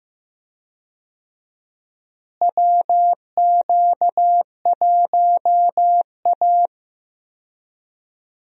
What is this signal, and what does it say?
Morse code, heard as a keyed tone, about 15 wpm, 700 Hz. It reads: WQ1A
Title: USA Call Signs - 4 Characters - Repeated 2x - 15wpm